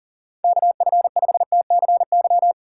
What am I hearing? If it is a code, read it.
Morse code, heard as a keyed tone, about 50 wpm, 700 Hz. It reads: KF5TCY
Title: Speed Racing - USA 2x3 Call Signs 40wpm